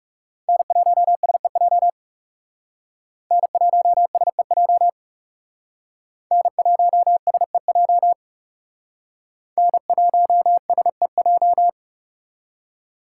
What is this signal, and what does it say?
Morse code, heard as a keyed tone, about 38 wpm, 700 Hz. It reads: N1SEJ N1SEJ N1SEJ N1SEJ